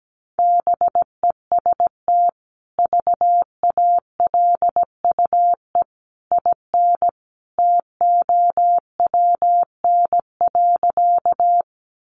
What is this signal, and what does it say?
Morse code, heard as a keyed tone, about 17 wpm, 700 Hz. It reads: BEST VALUE IN TOWN.
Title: Sentences from Top 600 Words 17wpm